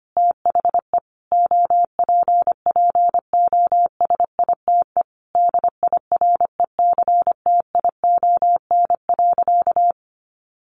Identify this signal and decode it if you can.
Morse code, heard as a keyed tone, about 25 wpm, 700 Hz. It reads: THE OPPOSITE DIRECTION.
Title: Sentences from Top 1300 Words 25wpm